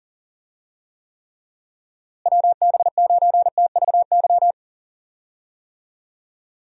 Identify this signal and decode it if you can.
Morse code, heard as a keyed tone, about 37 wpm, 700 Hz. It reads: WB9TVY
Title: USA Call Signs - 6 Characters - Repeated 2x - 40wpm